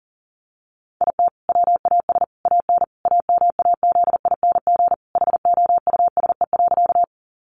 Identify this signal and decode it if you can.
Morse code, heard as a keyed tone, about 40 wpm, 700 Hz. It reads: IT WAS AN AMAZING HOUSE.
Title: Sentences from Top 1500 Words 40wpm